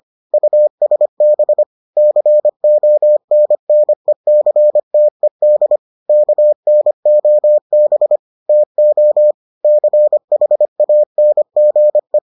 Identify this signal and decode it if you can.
Morse code, heard as a keyed tone, about 25 wpm, 600 Hz. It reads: USB CONNECTED KNOB TO CHANGE